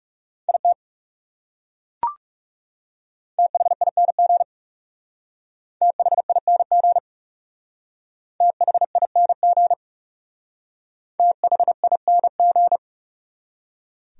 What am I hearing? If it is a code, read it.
Morse code, heard as a keyed tone, about 36 wpm, 700 Hz. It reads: IT THING THING THING THING
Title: Speed-Racing Top 100 Words 30wpm